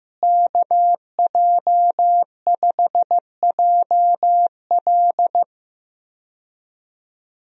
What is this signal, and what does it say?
Morse code, heard as a keyed tone, about 15 wpm, 700 Hz. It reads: KJ5JL
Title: USA Call Signs - 5 Characters - Repeated 3x - 15wpm